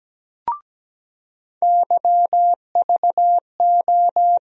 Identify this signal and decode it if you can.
Morse code, heard as a keyed tone, about 17 wpm, 700 Hz. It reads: YVO
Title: USA Call Sign Postfixes - 3 Letters 17wpm